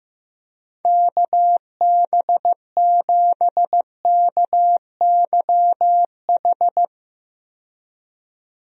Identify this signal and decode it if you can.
Morse code, heard as a keyed tone, about 15 wpm, 700 Hz. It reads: KB7KYH
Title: USA Call Signs - 6 Characters - 15wpm